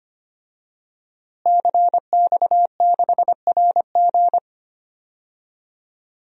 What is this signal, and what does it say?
Morse code, heard as a keyed tone, about 25 wpm, 700 Hz. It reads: CX6RG
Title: DXCC-100 Call Signs - 5 Characters - Repeated 2x - 25wpm